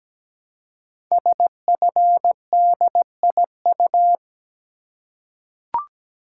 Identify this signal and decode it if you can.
Morse code, heard as a keyed tone, about 17 wpm, 700 Hz. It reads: SFDIU
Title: Code Groups - 5 Letters - 17wpm